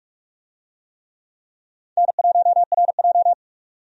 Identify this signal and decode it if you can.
Morse code, heard as a keyed tone, about 45 wpm, 700 Hz. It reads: N1RJ
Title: USA Call Signs - 4 Characters - 45wpm